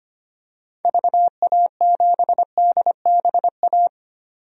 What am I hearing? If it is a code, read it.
Morse code, heard as a keyed tone, about 25 wpm, 700 Hz. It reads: VA7DBA
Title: DXCC-100 Call Signs - 6 Characters - 25wpm